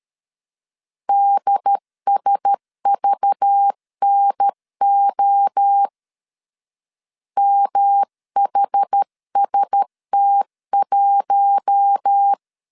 Morse code, heard as a keyed tone, about 13 wpm, 800 Hz. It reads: DSVNO MHST1
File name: CW / DET / 1 6 1500C S C 12wpm 3533